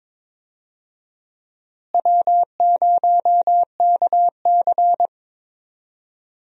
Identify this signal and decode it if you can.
Morse code, heard as a keyed tone, about 22 wpm, 700 Hz. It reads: W0KC
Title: USA Call Signs - 4 Characters - Repeated 3x - 22wpm